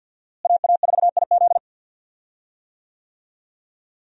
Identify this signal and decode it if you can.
Morse code, heard as a keyed tone, about 50 wpm, 700 Hz. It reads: AA4IZ
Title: USA Call Signs - 5 Characters - Repeated 3x - 50wpm